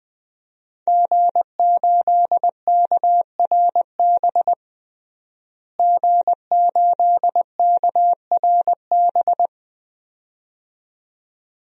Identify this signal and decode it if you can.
Morse code, heard as a keyed tone, about 20 wpm, 700 Hz. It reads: G8KRB G8KRB
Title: DXCC-100 Call Signs - 5 Characters - Repeated 2x - 20wpm